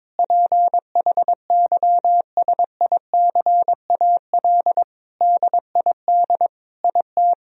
Morse code, heard as a keyed tone, about 21 wpm, 700 Hz. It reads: PHYSICAL DID IT
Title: Sentences from Top 1000 Words - Review Part 4 - 22wpm